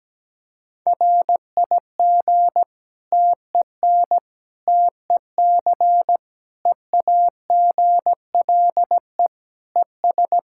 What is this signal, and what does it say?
Morse code, heard as a keyed tone, about 17 wpm, 700 Hz. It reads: RIG TEN TEC EAGLE ES
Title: Practice QSO - Exchange 2 - Info - 17wpm